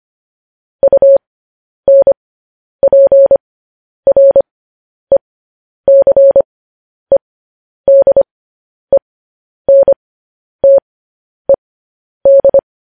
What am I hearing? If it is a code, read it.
Morse code, heard as a keyed tone, about 25 wpm, spaced as for 10 wpm, 550 Hz. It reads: UNPRECEDENTED
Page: news-headlines-in-morse-code-at-10-wpm